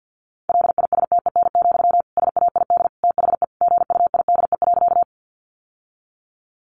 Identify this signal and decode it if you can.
Morse code, heard as a keyed tone, about 50 wpm, 700 Hz. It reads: LISTEN, SAID THE GUIDE.
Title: Sentences from Top 600 Words 50wpm